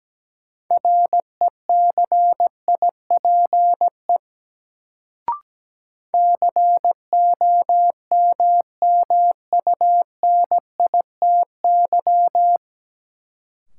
Morse code, heard as a keyed tone, about 17 wpm, 700 Hz. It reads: RECIPE COMMUNITY
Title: Sentences from Top 1200 Words - New Word Drill Alt - 17wpm